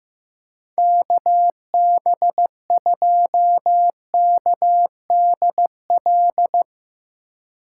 Morse code, heard as a keyed tone, about 15 wpm, 700 Hz. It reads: KB2KDL
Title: USA Call Signs - 6 Characters - 2x Character Spacing - 15wpm